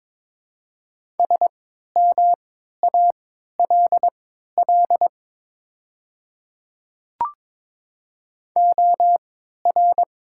Morse code, heard as a keyed tone, about 22 wpm, spaced as for 12 wpm, 690 Hz. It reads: SMALL OR
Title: 3x Character Spacing - Top 200 Words - 22wpm